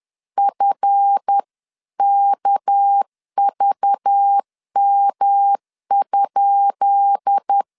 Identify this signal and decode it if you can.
Morse code, heard as a keyed tone, about 10 wpm, 800 Hz. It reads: FKVM?
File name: LEZIONI / 10 SIGNS / 10B INT 10wpm 2844